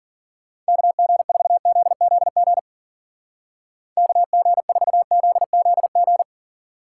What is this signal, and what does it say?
Morse code, heard as a keyed tone, about 52 wpm, 700 Hz. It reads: KG4ZZG KG4ZZG